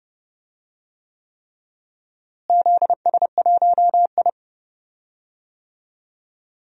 Morse code, heard as a keyed tone, about 30 wpm, 700 Hz. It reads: ZS1I
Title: DXCC-100 Call Signs - 4 Characters - Repeated 3x - 30wpm